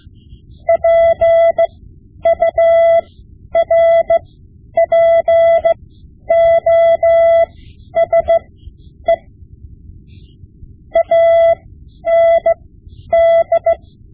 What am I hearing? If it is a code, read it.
Morse code, heard as a keyed tone, about 16 wpm, spaced as for 10 wpm, 670 Hz. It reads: PURPOSE AND
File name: Morse Lessons / Morse Lessons / Tape-5B